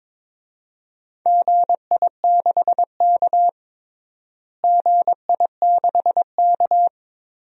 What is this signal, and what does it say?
Morse code, heard as a keyed tone, about 22 wpm, 700 Hz. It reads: GI6K GI6K
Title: DXCC-100 Call Signs - 4 Characters - Repeated 2x - 22wpm